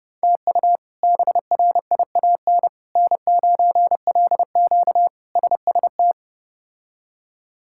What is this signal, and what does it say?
Morse code, heard as a keyed tone, about 30 wpm, 700 Hz. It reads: TU BRIAN N9LQ SST
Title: K1USN SST QSO Practice - 30wpm